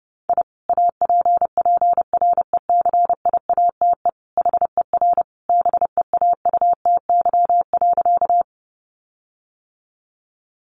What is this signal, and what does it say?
Morse code, heard as a keyed tone, about 30 wpm, 700 Hz. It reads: I APPRECIATE HER BEAUTY.
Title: Sentences from Top 1300 Words 30wpm